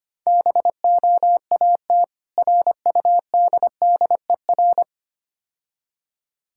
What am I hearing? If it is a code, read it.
Morse code, heard as a keyed tone, about 26 wpm, 700 Hz. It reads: BOAT RUDDER
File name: practice / 25wpm / Difficult Words - Speed-Racing with Hints - 25wpm